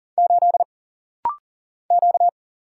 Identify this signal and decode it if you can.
Morse code, heard as a keyed tone, about 40 wpm, 700 Hz. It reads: 8 Q
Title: Single Letter-Number - Rapid-Fire 40wpm